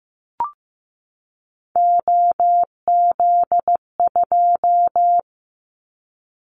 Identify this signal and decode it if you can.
Morse code, heard as a keyed tone, about 15 wpm, 700 Hz. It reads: OZ2
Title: Callsign Prefixes for Top 50 DXCC - 15wpm